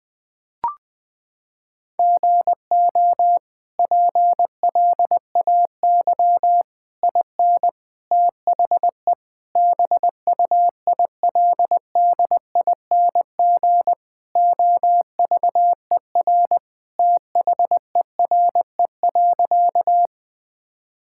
Morse code, heard as a keyed tone, about 20 wpm, 700 Hz. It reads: GO PLAY IN THE BUILDING OVER THERE.
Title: Sentences from Top 1000 Words - Review Part 5 - 20wpm